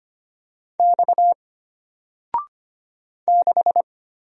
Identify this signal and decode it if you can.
Morse code, heard as a keyed tone, about 25 wpm, 700 Hz. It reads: X 6
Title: Single Letter-Number Alt 25wpm